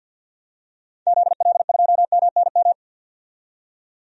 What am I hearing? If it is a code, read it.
Morse code, heard as a keyed tone, about 50 wpm, 700 Hz. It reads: ZL2MNM
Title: DXCC-100 Call Signs - 6 Characters - 50wpm